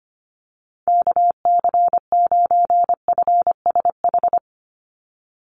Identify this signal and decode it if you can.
Morse code, heard as a keyed tone, about 25 wpm, 700 Hz. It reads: KC9FSH
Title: USA Call Signs 25wpm